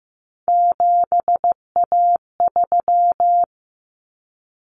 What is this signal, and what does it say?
Morse code, heard as a keyed tone, about 15 wpm, 700 Hz. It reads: 7A3
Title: Common Cut Numbers - 3-Digit 15wpm